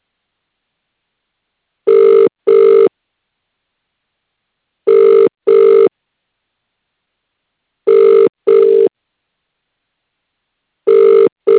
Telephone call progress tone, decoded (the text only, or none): ringback tone